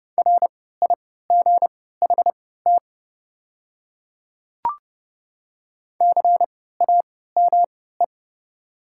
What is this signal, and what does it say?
Morse code, heard as a keyed tone, about 30 wpm, spaced as for 17 wpm, 700 Hz. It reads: RIGHT CAME
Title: 3x Character Spacing - Top 200 Words - 30wpm